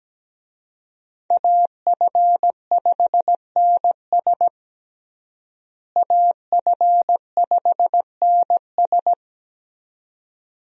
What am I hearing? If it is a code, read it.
Morse code, heard as a keyed tone, about 17 wpm, 700 Hz. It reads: AF5NS AF5NS
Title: USA Call Signs - 5 Characters - Repeated 2x - 17wpm